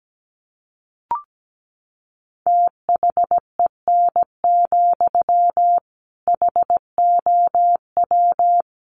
Morse code, heard as a keyed tone, about 17 wpm, 700 Hz. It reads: THEN, HOW
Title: Sentences from Top 100 Words 17wpm